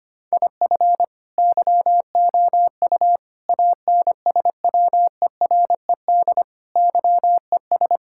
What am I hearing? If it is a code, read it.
Morse code, heard as a keyed tone, about 25 wpm, 700 Hz. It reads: IF YOU ANSWERED YES
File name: Sets of 9 Words - 25wpm